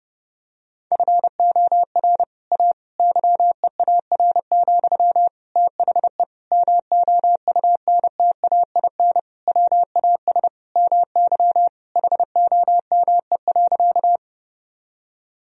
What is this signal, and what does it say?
Morse code, heard as a keyed tone, about 30 wpm, 700 Hz. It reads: FOR A YEAR, THE MOUNTAIN WAS MY HOME.
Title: Sentences from Top 1000 Words - Review Part 1 - 30wpm